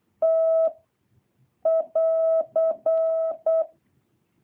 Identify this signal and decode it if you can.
Morse code, heard as a keyed tone, about 8 wpm, 660 Hz. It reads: T <AR>
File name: jh8cbh / 2023cw